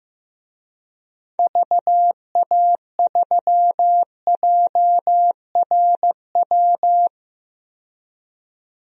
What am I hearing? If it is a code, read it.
Morse code, heard as a keyed tone, about 15 wpm, 700 Hz. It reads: VA3JRW
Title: DXCC-100 Call Signs - 6 Characters - 15wpm